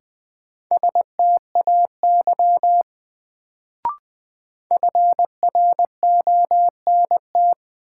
Morse code, heard as a keyed tone, about 20 wpm, 700 Hz. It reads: STAY FRONT
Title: Sentences from Top 500 Words - New Word Drill - 20wpm